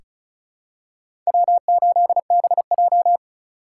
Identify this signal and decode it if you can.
Morse code, heard as a keyed tone, about 36 wpm, 700 Hz. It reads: W8BJ
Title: USA Call Signs - 4 Characters - 2x Character Spacing - 35wpm